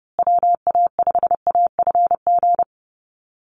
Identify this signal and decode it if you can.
Morse code, heard as a keyed tone, about 30 wpm, 700 Hz. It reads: WA5AFG
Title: USA Call Signs Repeated 2x 30wpm